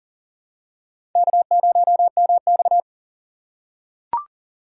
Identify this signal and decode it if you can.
Morse code, heard as a keyed tone, about 38 wpm, 700 Hz. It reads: K0MX
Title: USA Call Signs - 6x Character Spacing - 40wpm